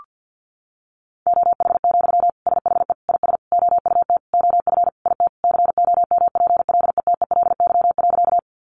Morse code, heard as a keyed tone, about 50 wpm, 700 Hz. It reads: OH, SHE IS OUT OF IT COMPLETELY.
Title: Sentences from Top 1100 Words 50wpm